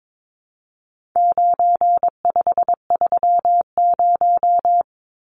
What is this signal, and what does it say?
Morse code, heard as a keyed tone, about 22 wpm, 700 Hz. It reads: 9530
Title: Numbers - 4-Digit 22wpm